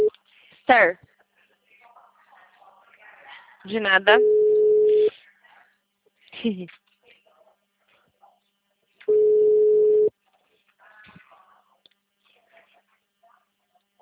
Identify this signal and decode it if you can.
Ringback tone (Europe, 425 Hz), with speech.
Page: Itu